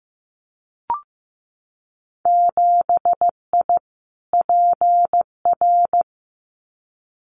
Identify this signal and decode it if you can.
Morse code, heard as a keyed tone, about 15 wpm, 700 Hz. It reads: 7I PR
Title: Winter Field Day Exchanges Encore 15wpm